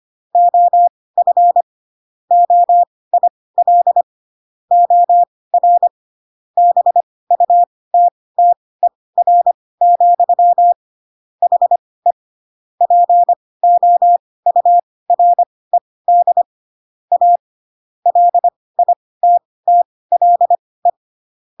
Morse code, heard as a keyed tone, about 25 wpm, spaced as for 18 wpm, 700 Hz. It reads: OF OIL OR BUTTER, HE POURED A LITTLE
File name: Pinocchio / Pinocchio ch5 12 25 ik1hge